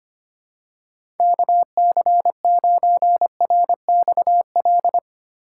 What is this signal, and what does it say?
Morse code, heard as a keyed tone, about 25 wpm, 700 Hz. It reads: KC9RXL
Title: USA Call Signs - 6 Characters - 4x Character Spacing - 25wpm